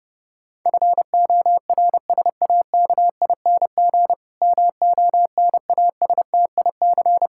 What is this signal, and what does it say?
Morse code, heard as a keyed tone, about 32 wpm, 700 Hz. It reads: FORSAKING MONASTIC
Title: Pangram Challenge v2 - 30wpm